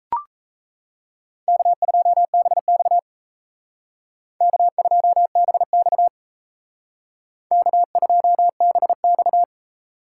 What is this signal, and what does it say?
Morse code, heard as a keyed tone, about 34 wpm, 700 Hz. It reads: K2BX K2BX K2BX
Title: Speed Racing - USA 1x2 Call Signs 28wpm